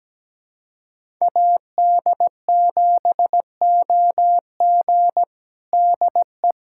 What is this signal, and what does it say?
Morse code, heard as a keyed tone, about 17 wpm, 700 Hz. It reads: AD7OG DE
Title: USA Callsign DE Callsign - 5 Characters - 17wpm